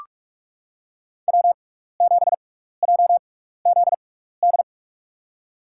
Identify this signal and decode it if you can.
Morse code, heard as a keyed tone, about 44 wpm, spaced as for 15 wpm, 700 Hz. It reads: W7JZD
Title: USA Call Signs - 5 Characters - 6x Character Spacing - 45wpm